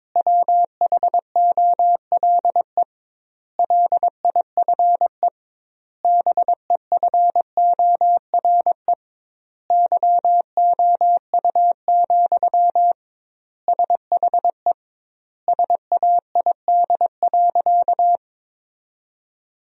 Morse code, heard as a keyed tone, about 22 wpm, 700 Hz. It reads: WHOLE LIFE BEFORE YOU, SHE SAID.
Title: 2x Word-Spacing - Sentences from Top 400 Words - 22wpm